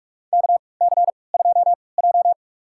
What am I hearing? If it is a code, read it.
Morse code, heard as a keyed tone, about 46 wpm, spaced as for 26 wpm, 700 Hz. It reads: KC2J
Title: USA Call Signs - 3x Character Spacing - 45wpm